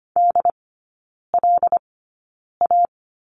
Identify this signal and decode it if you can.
Morse code, heard as a keyed tone, about 25 wpm, spaced as for 9 wpm, 700 Hz. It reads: DLA
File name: course / 25:10wpm / 093 Lesson 16e - Letter c - Three Characters 25wpm